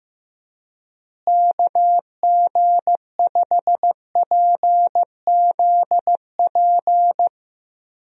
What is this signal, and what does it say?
Morse code, heard as a keyed tone, about 15 wpm, 700 Hz. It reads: KG5PZP
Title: USA Call Signs - 6 Characters - Repeated 3x - 15wpm